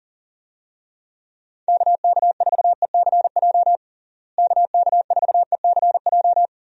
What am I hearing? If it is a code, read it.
Morse code, heard as a keyed tone, about 40 wpm, 700 Hz. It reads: KK4ECJ KK4ECJ
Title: USA Call Signs - 6 Characters - Repeated 3x - 40wpm